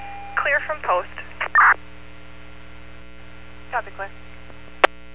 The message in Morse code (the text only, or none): none